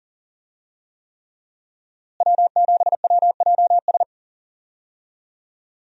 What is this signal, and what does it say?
Morse code, heard as a keyed tone, about 40 wpm, 700 Hz. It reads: W7WJS
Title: USA Call Signs - 5 Characters - Repeated 3x - 40wpm